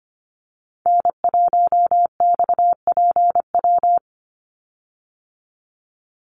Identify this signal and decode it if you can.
Morse code, heard as a keyed tone, about 25 wpm, 700 Hz. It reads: N1XPW
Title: USA Call Signs 25wpm